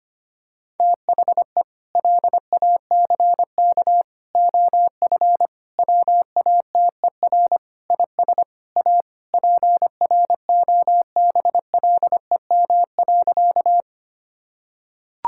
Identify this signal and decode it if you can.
Morse code, heard as a keyed tone, about 25 wpm, 700 Hz. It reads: THE LACK OF WATER IS A PROBLEM.
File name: practice / 25wpm / Sentences with 3-Word Phrases - 25wpm